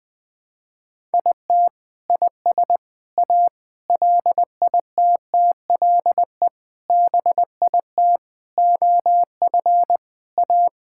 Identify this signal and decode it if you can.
Morse code, heard as a keyed tone, about 20 wpm, 700 Hz. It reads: IT IS A LITTLE BIT OF A
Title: Sentences with 5-Word Phrases - 20wpm